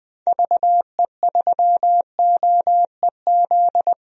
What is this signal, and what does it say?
Morse code, heard as a keyed tone, about 20 wpm, 700 Hz. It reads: VE3OEZ
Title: DXCC-100 Call Signs - 6 Characters - 20wpm